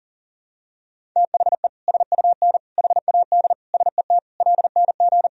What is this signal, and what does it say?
Morse code, heard as a keyed tone, about 40 wpm, 700 Hz. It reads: THE SUN HAD SET LNG
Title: Sentences from Top 300 Words and Abbreviations 40wpm